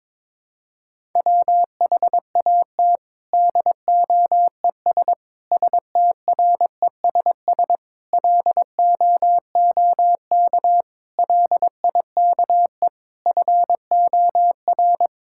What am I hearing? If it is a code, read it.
Morse code, heard as a keyed tone, about 21 wpm, 700 Hz. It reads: WHAT DOES STRESS LOOK LIKE FOR
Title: Sentences from Top 1000 Words - Review Part 6 - 22wpm